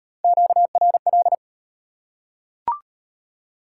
Morse code, heard as a keyed tone, about 40 wpm, 690 Hz. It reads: QRL